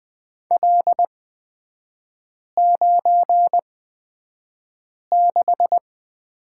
Morse code, heard as a keyed tone, about 20 wpm, spaced as for 5 wpm, 700 Hz. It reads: L96
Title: 224 Lesson 36e - BT - Three Characters 20wpm